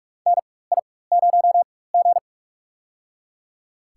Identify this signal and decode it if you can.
Morse code, heard as a keyed tone, about 44 wpm, spaced as for 21 wpm, 700 Hz. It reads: NI0G
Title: USA Call Signs - 4 Characters - 4x Character Spacing - 45wpm